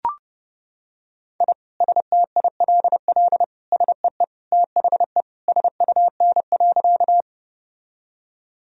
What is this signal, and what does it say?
Morse code, heard as a keyed tone, about 30 wpm, 700 Hz. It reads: I STILL SEE THE SUN.
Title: Sentences from Top 1000 Words - Review Part 1 - 30wpm